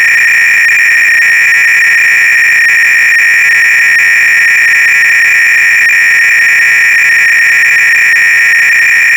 300-baud modem data.
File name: ACR-Audio-Cassette-Programs / CharlesSwiger-Donations / GOBLINS cload-g